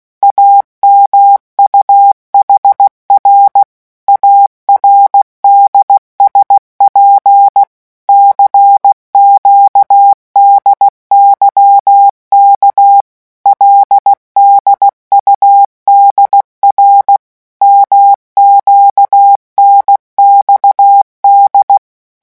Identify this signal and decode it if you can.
Morse code, heard as a keyed tone, about 16 wpm, 800 Hz. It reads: AMUHR ARDSP CQDYK LDUDR MQNXD
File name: morse mp3 / revision5-vitesse 15 mots